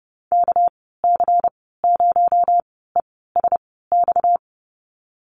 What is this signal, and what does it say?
Morse code, heard as a keyed tone, about 30 wpm, spaced as for 17 wpm, 700 Hz. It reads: KC0ESX
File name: Lesson 37g - Letter x - Call Signs 30wpm